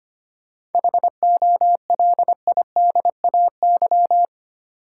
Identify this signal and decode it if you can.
Morse code, heard as a keyed tone, about 25 wpm, 700 Hz. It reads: HOLIDAY
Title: Seven Letter Words - 25wpm